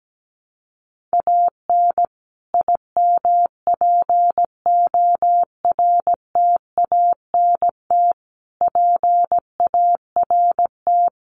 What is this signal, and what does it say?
Morse code, heard as a keyed tone, about 17 wpm, 700 Hz. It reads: AN IMPORTANT PART
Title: Sets of 5 Words Encore 17wpm